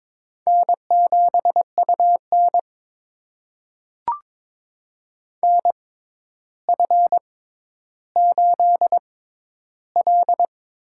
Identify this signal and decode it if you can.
Morse code, heard as a keyed tone, about 22 wpm, spaced as for 11 wpm, 700 Hz. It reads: N7UN NF8L
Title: USA Call Signs - 6x Character Spacing - 22wpm